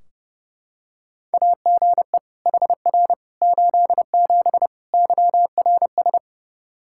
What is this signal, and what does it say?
Morse code, heard as a keyed tone, about 30 wpm, 700 Hz. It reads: AGE HR 87 YRS
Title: Practice QSO - Exchange 3 - More Info - 30wpm